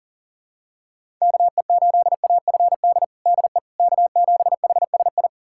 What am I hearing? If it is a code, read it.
Morse code, heard as a keyed tone, about 40 wpm, 700 Hz. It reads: KE8AFD DE K7HSI
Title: USA Callsign DE Callsign 40wpm